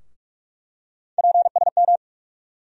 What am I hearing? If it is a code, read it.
Morse code, heard as a keyed tone, about 43 wpm, 700 Hz. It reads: PSM
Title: USA Call Sign Postfixes - 3 Letters 45wpm